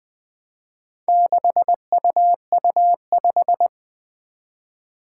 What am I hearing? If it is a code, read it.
Morse code, heard as a keyed tone, about 20 wpm, 700 Hz. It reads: 6UU5
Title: Four Challenging Letter-Number Combinations 20wpm